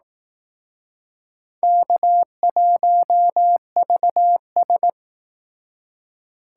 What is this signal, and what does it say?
Morse code, heard as a keyed tone, about 20 wpm, 700 Hz. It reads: K1VS